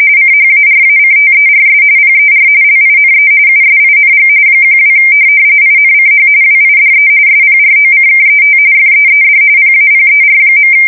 RTTY, about 45.45 baud. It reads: CHARACTERS PER LINE USING 12 INCH WIDE SPROCKET FEED PAPER.
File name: fsk170 / teletype-corp